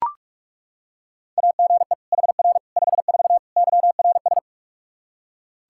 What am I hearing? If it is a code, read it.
Morse code, heard as a keyed tone, about 41 wpm, 700 Hz. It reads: AGE HR 54 YRS
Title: Practice QSO - Exchange 3 - More Info - 45wpm